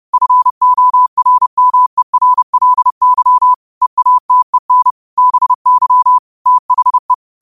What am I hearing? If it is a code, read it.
Morse code, heard as a keyed tone, about 30 wpm, 1000 Hz. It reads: FORMERLY EATEN BY THE
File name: 30wpm